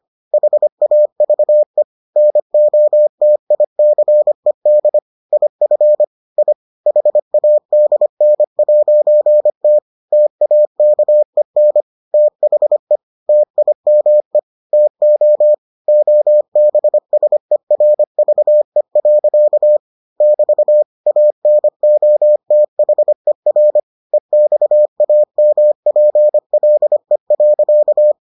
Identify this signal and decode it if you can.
Morse code, heard as a keyed tone, about 25 wpm, 600 Hz. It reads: HAVE NOTICED IF I HADN'T TAKEN THE TIME TO OBSERVE. <BT> ANOTHER EXAMPLE.